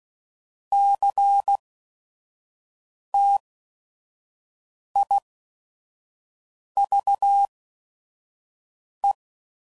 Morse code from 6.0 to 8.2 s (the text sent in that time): V